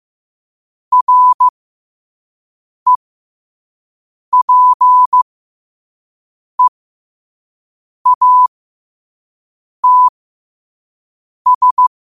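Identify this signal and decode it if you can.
Morse code, heard as a keyed tone, about 15 wpm, spaced as for 5 wpm, 1000 Hz. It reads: REPEATS
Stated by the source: Morse code practice at 5 wpm.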